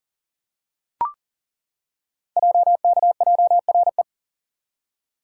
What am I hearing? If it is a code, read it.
Morse code, heard as a keyed tone, about 42 wpm, 700 Hz. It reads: JKJRE
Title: Code Groups - 5 Letters - 40wpm